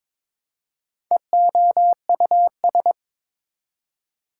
Morse code, heard as a keyed tone, about 22 wpm, 700 Hz. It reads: EOUS
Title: Prefixes and Suffixes 22wpm